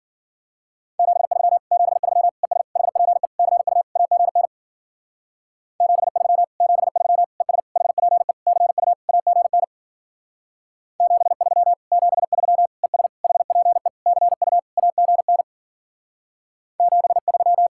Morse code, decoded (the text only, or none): 73 73 ES HPE CU AGN 73 73 ES HPE CU AGN 73 73 ES HPE CU AGN 73